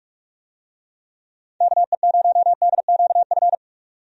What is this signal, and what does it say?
Morse code, heard as a keyed tone, about 45 wpm, 700 Hz. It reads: KE0DQF
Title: USA Call Signs - 6 Characters - 45wpm